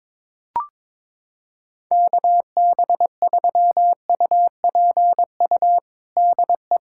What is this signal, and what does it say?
Morse code, heard as a keyed tone, about 22 wpm, 700 Hz. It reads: KB3UPU DE
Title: USA Callsign DE Callsign 22wpm